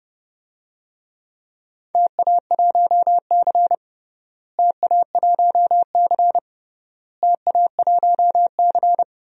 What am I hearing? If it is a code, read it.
Morse code, heard as a keyed tone, about 30 wpm, 700 Hz. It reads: TA1C TA1C TA1C
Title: DXCC-100 Call Signs - 4 Characters - Repeated 3x - 30wpm